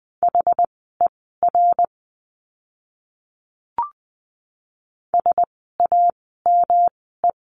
Morse code, heard as a keyed tone, about 20 wpm, spaced as for 15 wpm, 700 Hz. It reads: HER SAME
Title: 230 Lesson 37f - Letter x - Words 20wpm